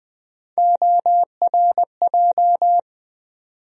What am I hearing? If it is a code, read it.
Morse code, heard as a keyed tone, about 20 wpm, 700 Hz. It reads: ORJ
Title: USA Call Sign Postfixes - 3 Letters 20wpm